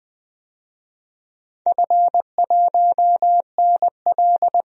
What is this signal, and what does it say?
Morse code, heard as a keyed tone, about 20 wpm, 700 Hz. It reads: F1NL
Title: DXCC-100 Call Signs - 4 Characters - 20wpm